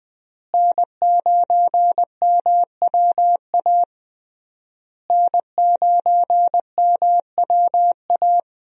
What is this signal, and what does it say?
Morse code, heard as a keyed tone, about 20 wpm, 700 Hz. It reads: N9MWA N9MWA
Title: USA Call Signs - 5 Characters - Repeated 3x - 20wpm